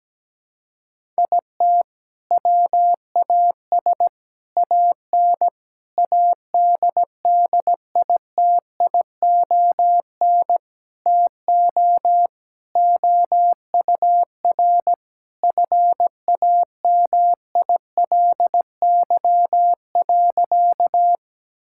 Morse code, heard as a keyed tone, about 17 wpm, 700 Hz. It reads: IT WAS AN ADDITION TO OUR FAMILY.
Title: Sentences from Top 1000 Words - Review Part 6 - 17wpm